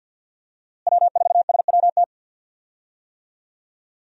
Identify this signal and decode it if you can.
Morse code, heard as a keyed tone, about 46 wpm, 700 Hz. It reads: W4SWT
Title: USA Call Signs - 5 Characters - Repeated 2x - 50wpm